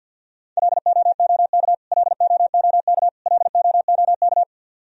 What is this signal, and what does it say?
Morse code, heard as a keyed tone, about 51 wpm, 700 Hz. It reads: LOOK LOOK LOOK
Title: Top 100 Words Repeated 50wpm